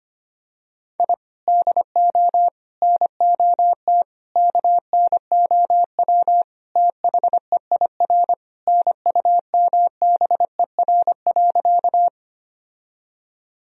Morse code, heard as a keyed tone, about 25 wpm, 700 Hz. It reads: I DO NOT KNOW THEIR NUMBER.
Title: Sentences from Top 1000 Words - Review Part 1 - 25wpm